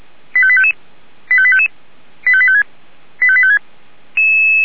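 Five-tone selective call (ZVEI), heard as quick runs of five tones, tones of about 70 ms. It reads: 85855 85855 85856 85856
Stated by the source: ZVEI five-tone selective call received by SDR.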